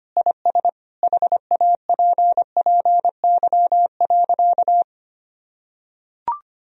Morse code, heard as a keyed tone, about 25 wpm, 700 Hz. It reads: IS HAPPY.
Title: Sentences from Top 1000 Words - Review Part 3 - 25wpm